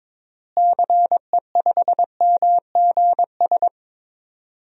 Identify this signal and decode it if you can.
Morse code, heard as a keyed tone, about 21 wpm, 700 Hz. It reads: CE5MGS
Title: DXCC-100 Call Signs - 6 Characters - Repeated 3x - 22wpm